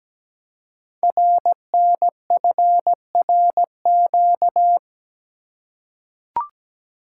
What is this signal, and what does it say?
Morse code, heard as a keyed tone, about 17 wpm, 700 Hz. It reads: RNFRQ
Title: Code Groups - 5 Letters - 17wpm